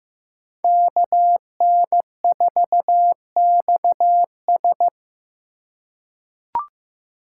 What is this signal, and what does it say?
Morse code, heard as a keyed tone, about 15 wpm, 700 Hz. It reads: KN4XS
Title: USA Call Signs - 3x Character Spacing - 15wpm